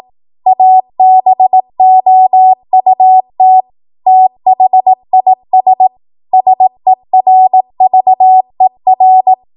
Morse code, heard as a keyed tone, about 18 wpm, 750 Hz. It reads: ABOUT THIS SERVER
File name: Archive / 18 WPM / 18WPM